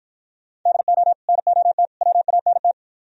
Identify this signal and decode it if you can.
Morse code, heard as a keyed tone, about 53 wpm, 700 Hz. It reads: DO NOT WANT